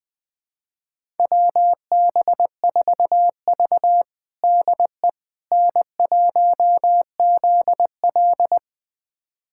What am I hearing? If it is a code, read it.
Morse code, heard as a keyed tone, about 20 wpm, 700 Hz. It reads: WB4V DE N1ZL
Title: USA Callsign DE Callsign - 4 Characters - 20wpm